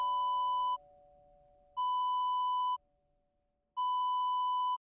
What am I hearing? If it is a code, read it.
There is no code.